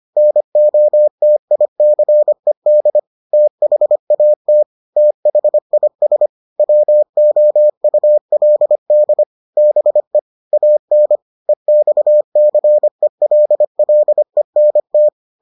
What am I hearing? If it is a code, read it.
Morse code, heard as a keyed tone, about 25 wpm, 600 Hz. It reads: NOTICED THAT THIS WOULD BE AN EXCELLENT